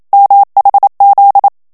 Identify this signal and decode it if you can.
Morse code, heard as a keyed tone, about 27 wpm, 800 Hz. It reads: MHZ
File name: HOLLANDAIS texte clair 1 vitesse 26 mots minute